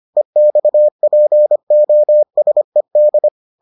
Morse code, heard as a keyed tone, about 25 wpm, 600 Hz. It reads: EXPOSED